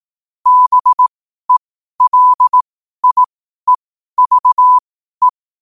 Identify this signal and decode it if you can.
Morse code, heard as a keyed tone, about 18 wpm, spaced as for 12 wpm, 1000 Hz. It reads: BELIEVE
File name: qotdfiles / 12wpm